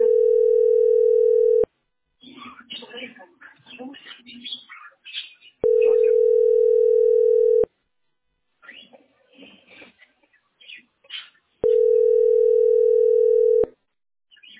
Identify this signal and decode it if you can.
Ringback tone (North America), 440 + 480 Hz, with speech.
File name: 3f14cd75-226b-4dda-849e